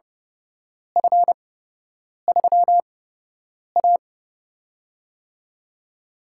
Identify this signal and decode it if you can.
Morse code, heard as a keyed tone, about 31 wpm, spaced as for 8 wpm, 700 Hz. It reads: F3A
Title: USA Call Signs - 4 Characters - 8x Character Spacing - 30wpm